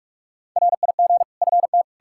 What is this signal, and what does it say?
Morse code, heard as a keyed tone, about 47 wpm, 700 Hz. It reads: RIG FT
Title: Practice QSO - Exchange 2 - Info - 45wpm